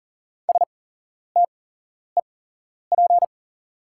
Morse code, heard as a keyed tone, about 39 wpm, spaced as for 11 wpm, 690 Hz. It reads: STEP
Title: 8x Character Spacing - Top 500 Words - 40wpm